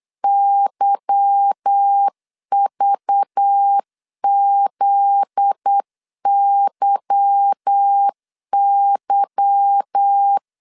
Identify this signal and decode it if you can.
Morse code, heard as a keyed tone, about 8 wpm, 800 Hz. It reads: YVZYY
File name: LEZIONI / 8 YJ45 / 8 Y 8wpm 3245